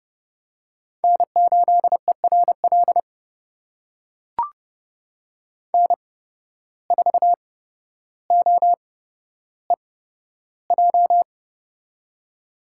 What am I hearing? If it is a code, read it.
Morse code, heard as a keyed tone, about 29 wpm, spaced as for 13 wpm, 700 Hz. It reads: N8ERL N4OEJ
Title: USA Call Signs - 5 Characters - 8x Character Spacing - 30wpm